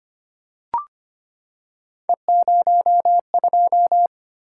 Morse code, heard as a keyed tone, about 25 wpm, 700 Hz. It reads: E02
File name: Three Letter-Number Combinations - 25wpm